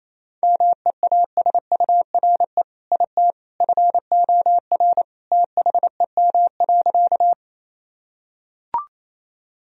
Morse code, heard as a keyed tone, about 27 wpm, 700 Hz. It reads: MEASURE IT FOR THEM.
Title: Sentences from Top 1000 Words - Review Part 2 - 28wpm